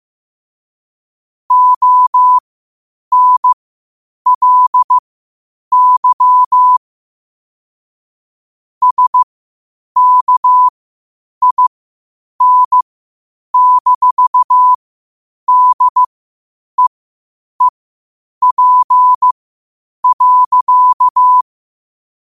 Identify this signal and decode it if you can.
Morse code, heard as a keyed tone, about 15 wpm, spaced as for 8 wpm, 1000 Hz. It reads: ONLY SKIN-DEEP.
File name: qotdfiles / 8wpm